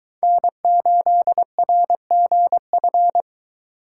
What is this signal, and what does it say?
Morse code, heard as a keyed tone, about 22 wpm, 700 Hz. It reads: N8RGF